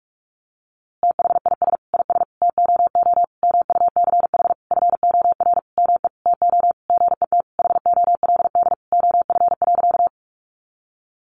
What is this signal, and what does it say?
Morse code, heard as a keyed tone, about 46 wpm, 700 Hz. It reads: THIS IS TOO MUCH FOR ME TO GET HOLD OF.
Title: A Sound Of Thunder 45wpm